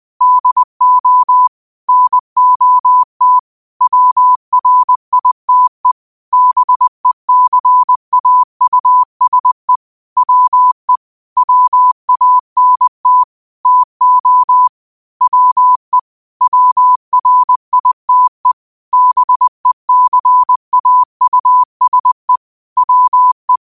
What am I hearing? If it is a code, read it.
Morse code, heard as a keyed tone, about 20 wpm, 1000 Hz. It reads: DO NOT WRITE BECAUSE WE WANT TO WE WRITE BECAUSE WE